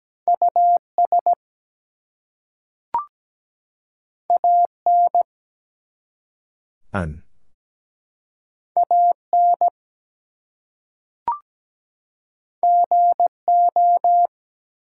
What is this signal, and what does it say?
Morse code, heard as a keyed tone, about 17 wpm, 700 Hz. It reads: US AN AN GO
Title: Two Letter Words - 17wpm